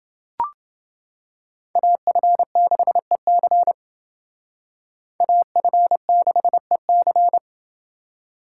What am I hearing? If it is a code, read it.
Morse code, heard as a keyed tone, about 28 wpm, 700 Hz. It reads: AF6EC AF6EC